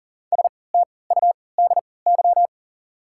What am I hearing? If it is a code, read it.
Morse code, heard as a keyed tone, about 40 wpm, spaced as for 23 wpm, 700 Hz. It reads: STUDY
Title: 3x Character Spacing - Top 300 Words - 40wpm